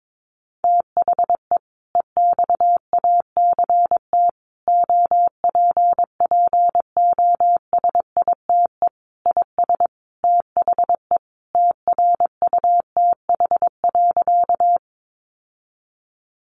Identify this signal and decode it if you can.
Morse code, heard as a keyed tone, about 22 wpm, 700 Hz. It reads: THE EXACT OPPOSITE IS THE TRUTH.
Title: Sentences from Top 1300 Words 22wpm